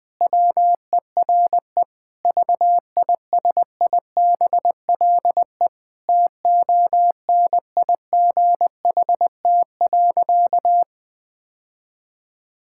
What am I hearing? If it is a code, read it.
Morse code, heard as a keyed tone, about 20 wpm, 700 Hz. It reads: WERE VISIBLE TONIGHT.
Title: Sentences with 2-Word Phrases - Spoken Beginning - 20wpm